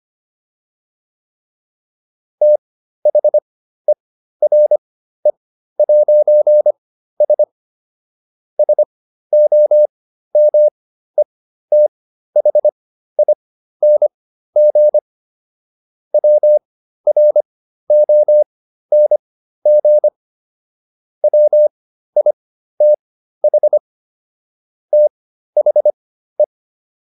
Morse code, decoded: THERE'S SOMETHING WRONG WITH THE

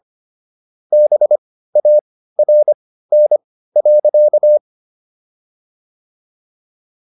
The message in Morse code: BARN.

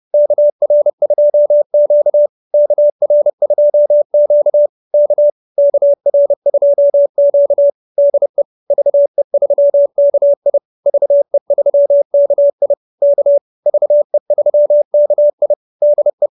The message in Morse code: KR2Q KR2Q K KR2Q DE VE3KI VE3KI K VE3KI DE

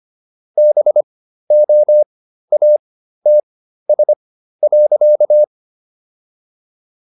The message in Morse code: BOATS.

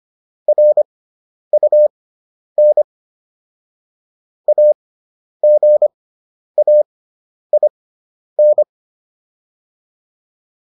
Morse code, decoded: RUN AGAIN

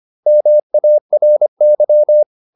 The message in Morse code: MARY

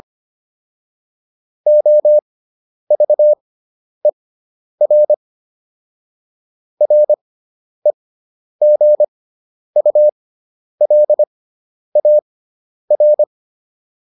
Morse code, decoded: OVER REGULAR

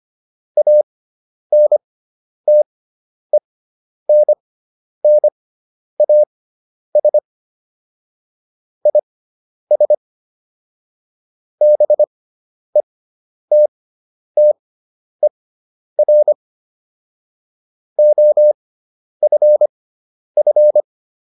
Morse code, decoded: ANTENNAS IS BETTER OFF

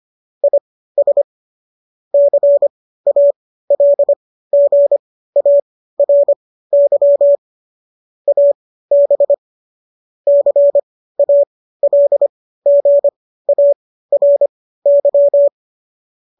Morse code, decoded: IS CALGARY AB CALGARY

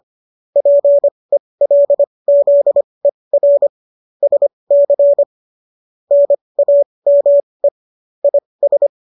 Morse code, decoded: PELZER SC NAME IS

